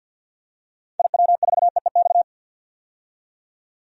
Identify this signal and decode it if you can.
Morse code, heard as a keyed tone, about 51 wpm, 700 Hz. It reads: IW4EEX
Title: DXCC-100 Call Signs - 6 Characters - 50wpm